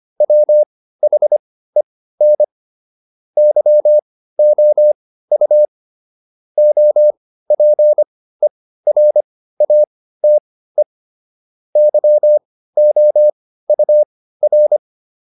Morse code, heard as a keyed tone, about 25 wpm, spaced as for 15 wpm, 600 Hz. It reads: WHEN YOU OPERATE YOUR